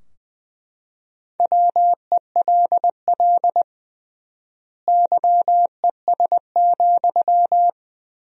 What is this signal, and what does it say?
Morse code, heard as a keyed tone, about 20 wpm, 700 Hz. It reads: WELL YES,
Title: 3x Word-Spacing - Sentences from Top 500 Words - 20wpm